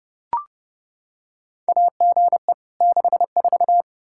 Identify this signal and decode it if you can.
Morse code, heard as a keyed tone, about 30 wpm, 700 Hz. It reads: AGE 64
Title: Practice QSO - Exchange 3 - More Info - 30wpm